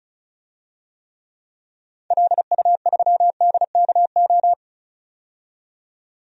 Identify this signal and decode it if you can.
Morse code, heard as a keyed tone, about 34 wpm, 700 Hz. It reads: LU3DKO
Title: DXCC-100 Call Signs - 6 Characters - 35wpm